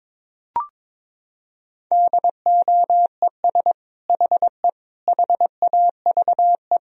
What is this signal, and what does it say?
Morse code, heard as a keyed tone, about 22 wpm, 700 Hz. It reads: DOES HE HAVE
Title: Sentences from Top 1000 Words - Review Part 3 - 22wpm